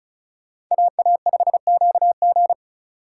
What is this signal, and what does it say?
Morse code, heard as a keyed tone, about 46 wpm, 700 Hz. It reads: AA5QG